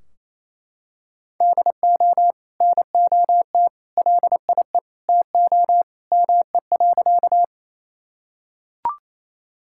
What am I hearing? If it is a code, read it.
Morse code, heard as a keyed tone, about 28 wpm, 700 Hz. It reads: DO NOT LIE TO ME.
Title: Sentences from Top 1600 Words 28wpm